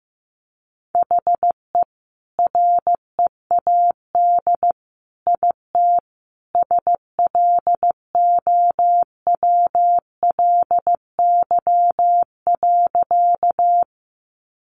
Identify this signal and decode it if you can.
Morse code, heard as a keyed tone, about 15 wpm, 700 Hz. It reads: HE READ IT SLOWLY.
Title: Sentences from Top 1100 Words 15wpm